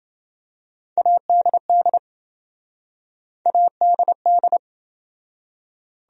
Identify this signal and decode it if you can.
Morse code, heard as a keyed tone, about 26 wpm, 700 Hz. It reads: ADD ADD